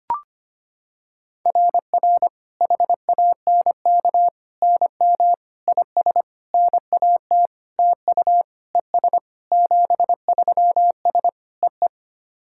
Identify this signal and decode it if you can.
Morse code, heard as a keyed tone, about 25 wpm, 700 Hz. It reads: RR HANK NM IS NAT TU ES 73S EE
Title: SOTA QSO Practice 25wpm